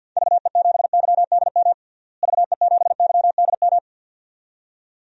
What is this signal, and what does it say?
Morse code, heard as a keyed tone, about 52 wpm, 700 Hz. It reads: VE7YDM VE7YDM
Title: DXCC-100 Call Signs - 6 Characters - Repeated 2x - 50wpm